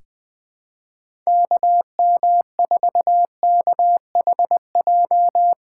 Morse code, heard as a keyed tone, about 21 wpm, 700 Hz. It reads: KM4KHJ
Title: USA Call Signs - 6 Characters - 6x Character Spacing - 20wpm